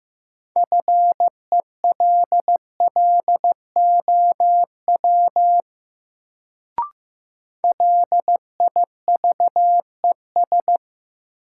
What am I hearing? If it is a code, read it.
Morse code, heard as a keyed tone, about 15 wpm, 700 Hz. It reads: FELLOW LIVES
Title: Sets of 5 Words Encore - New Word Drill - 15wpm